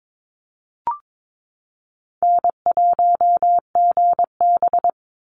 Morse code, heard as a keyed tone, about 22 wpm, 700 Hz. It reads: N1GB
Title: NA SOTA Chaser Callsigns 22wpm